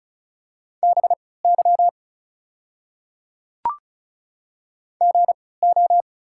Morse code, heard as a keyed tone, about 36 wpm, spaced as for 20 wpm, 700 Hz. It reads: BY GO
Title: 3x Character Spacing - Top 200 Words - 35wpm